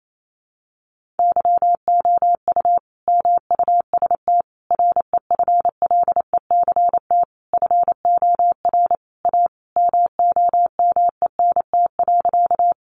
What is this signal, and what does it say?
Morse code, heard as a keyed tone, about 28 wpm, 700 Hz. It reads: YOU MUST REFLECT FOR A MOMENT.
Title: Sentences from Top 1500 Words 28wpm